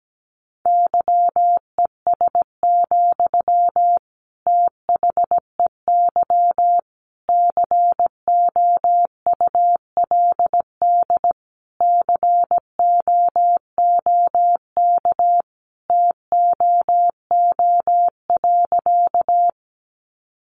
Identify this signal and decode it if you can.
Morse code, heard as a keyed tone, about 17 wpm, 700 Hz. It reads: YES, THEY COULD COOK TOO.
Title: Sentences from Top 700 Words 17wpm